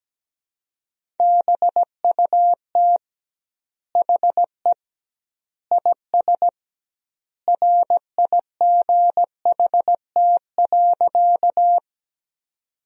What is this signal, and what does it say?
Morse code, heard as a keyed tone, about 17 wpm, 700 Hz. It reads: BUT HE IS RIGHT.
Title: 2x Word-Spacing - Sentences from Top 200 Words - 17wpm